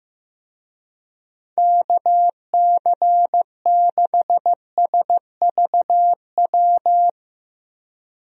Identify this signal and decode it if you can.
Morse code, heard as a keyed tone, about 16 wpm, 700 Hz. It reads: KC6SVW
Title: USA Call Signs - 6 Characters - Repeated 2x - 15wpm